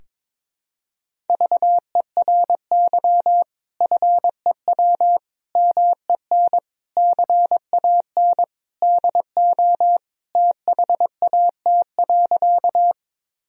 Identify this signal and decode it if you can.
Morse code, heard as a keyed tone, about 22 wpm, 700 Hz. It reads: VERY FEW MEN CAN DO THAT.
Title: Sentences from Top 1000 Words - Review Part 1 - 22wpm